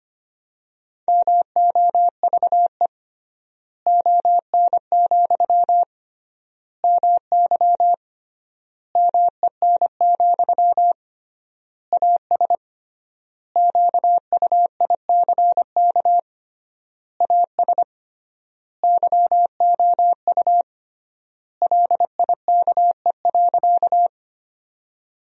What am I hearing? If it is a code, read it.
Morse code, heard as a keyed tone, about 25 wpm, 700 Hz. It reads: MOVE ON, MY MEN, AS QUICK AS YOU LIKE.
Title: 3x Word-Spacing - Sentences from Top 500 Words - 25wpm